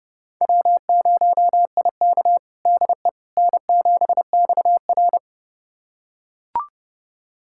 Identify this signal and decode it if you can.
Morse code, heard as a keyed tone, about 30 wpm, 700 Hz. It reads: W0IK DE N7XR
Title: USA Callsign DE Callsign - 4 Characters - 30wpm